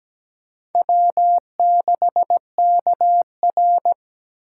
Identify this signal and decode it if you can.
Morse code, heard as a keyed tone, about 20 wpm, 700 Hz. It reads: W6KR
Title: Speed Racing - USA 1x2 Call Signs 17wpm